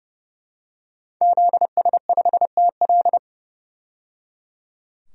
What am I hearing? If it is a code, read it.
Morse code, heard as a keyed tone, about 31 wpm, 700 Hz. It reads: ZS5TL
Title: DXCC-100 Call Signs - 5 Characters - 30wpm